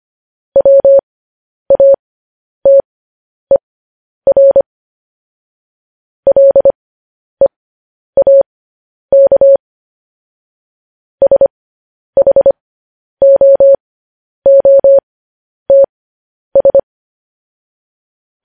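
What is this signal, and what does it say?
Morse code, heard as a keyed tone, about 25 wpm, spaced as for 10 wpm, 550 Hz. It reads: WATER LEAK SHOOTS
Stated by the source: Morse code news headlines sent at 10.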